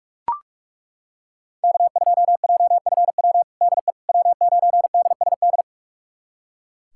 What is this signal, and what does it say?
Morse code, heard as a keyed tone, about 43 wpm, 700 Hz. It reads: K2JFW DE W9DSD
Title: USA Callsign DE Callsign - 5 Characters - 45wpm